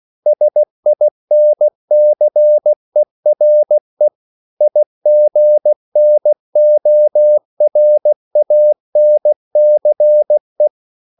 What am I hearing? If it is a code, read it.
Morse code, heard as a keyed tone, about 16 wpm, 600 Hz. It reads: SINCERE IGNORANCE
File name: CW / Practice / EnglishQuotations0116